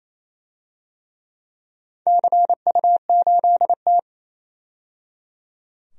Morse code, heard as a keyed tone, about 28 wpm, 700 Hz. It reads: CU8T
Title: DXCC-100 Call Signs - 4 Characters - 28wpm